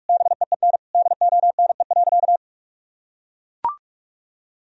Morse code, heard as a keyed tone, about 41 wpm, 700 Hz. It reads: BEEN DONE.
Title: Sentences from Top 1000 Words - Review Part 1 - 45wpm